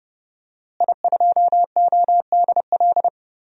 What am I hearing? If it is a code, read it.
Morse code, heard as a keyed tone, about 30 wpm, 700 Hz. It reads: I2ODL
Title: DXCC-100 Call Signs - 5 Characters - Repeated 3x - 30wpm